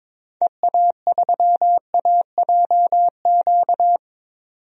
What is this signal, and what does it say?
Morse code, heard as a keyed tone, about 22 wpm, 700 Hz. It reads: EA3AJQ
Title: DXCC-100 Call Signs - 6 Characters - 22wpm